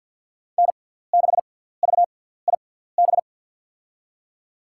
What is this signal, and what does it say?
Morse code, heard as a keyed tone, about 47 wpm, spaced as for 17 wpm, 700 Hz. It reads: N6VIB
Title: USA Call Signs - 5 Characters - 6x Character Spacing - 50wpm